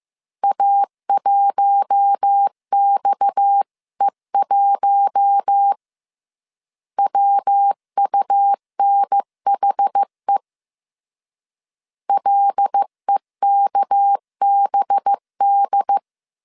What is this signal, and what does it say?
Morse code, heard as a keyed tone, about 15 wpm, 800 Hz. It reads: A1XE1 WUNHE LEKBD
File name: CW / DET / 1 6 1500C S C 14wpm 2951